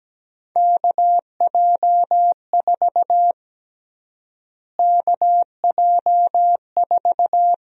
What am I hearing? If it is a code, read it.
Morse code, heard as a keyed tone, about 17 wpm, 700 Hz. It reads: KJ4 KJ4
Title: USA Call Sign Prefixes - Repeated 2x - 17wpm